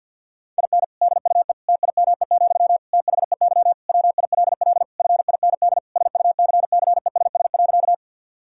Morse code, heard as a keyed tone, about 50 wpm, 700 Hz. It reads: IN DUE TIME, THEY WILL FIND SUCCESS.
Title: Sentences from Top 1000 Words - Review Part 6 - 50wpm